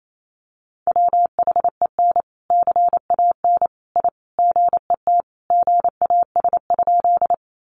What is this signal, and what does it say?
Morse code, heard as a keyed tone, about 28 wpm, 700 Hz. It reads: WHEN CAN I GET GAS?
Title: Sentences from Top 1100 Words 28wpm